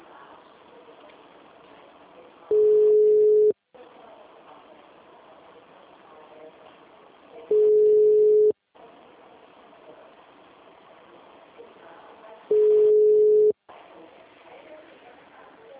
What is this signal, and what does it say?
Ringback tone (Europe, 425 Hz).